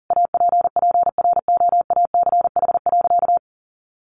Morse code, heard as a keyed tone, about 40 wpm, 700 Hz. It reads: APPROACH.
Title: Sentences from Top 1400 Words 40wpm